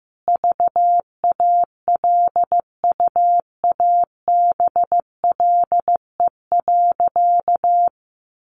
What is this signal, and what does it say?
Morse code, heard as a keyed tone, about 15 wpm, 700 Hz. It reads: VALUABLE.
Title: Sentences from Top 1400 Words 15wpm